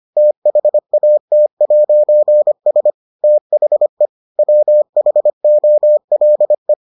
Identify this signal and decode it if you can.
Morse code, heard as a keyed tone, about 25 wpm, 600 Hz. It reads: THAT'S THE WHOLE